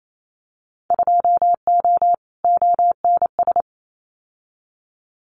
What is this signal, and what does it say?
Morse code, heard as a keyed tone, about 28 wpm, 700 Hz. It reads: 2O ONS
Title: Winter Field Day Exchanges Encore 28wpm